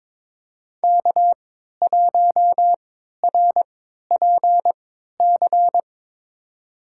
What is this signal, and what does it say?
Morse code, heard as a keyed tone, about 22 wpm, spaced as for 12 wpm, 700 Hz. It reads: K1RPC
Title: USA Call Signs - 5 Characters - 3x Character Spacing - 22wpm